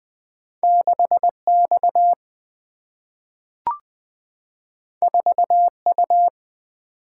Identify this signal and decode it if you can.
Morse code, heard as a keyed tone, about 20 wpm, 700 Hz. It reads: 6X 4U
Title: Single-Letter Single-Number 20wpm